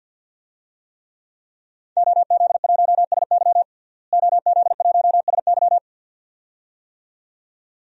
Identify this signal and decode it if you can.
Morse code, heard as a keyed tone, about 50 wpm, 700 Hz. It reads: OZ1SY OZ1SY
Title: DXCC-100 Call Signs - 5 Characters - Repeated 2x - 50wpm